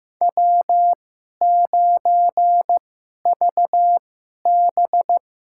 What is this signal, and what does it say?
Morse code, heard as a keyed tone, about 15 wpm, spaced as for 11 wpm, 700 Hz. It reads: W9VB
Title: USA Call Signs - 4 Characters - 2x Character Spacing - 15wpm